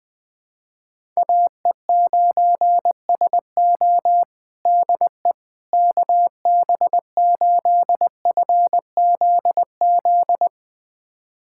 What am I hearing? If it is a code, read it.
Morse code, heard as a keyed tone, about 20 wpm, 700 Hz. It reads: AE9SO DE KB8FZZ
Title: USA Callsign DE Callsign 20wpm